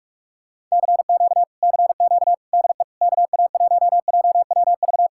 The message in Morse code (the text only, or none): CQ CQ DE KA1JWV